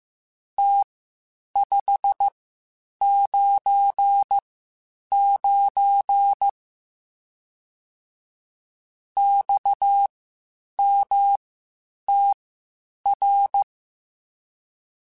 Morse code, heard as a keyed tone, about 15 wpm, spaced as for 8 wpm, 800 Hz. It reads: T599 XMTR
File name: (spaced) / QSOs Spaced-mp3s / WA7QOX 1 @13 sp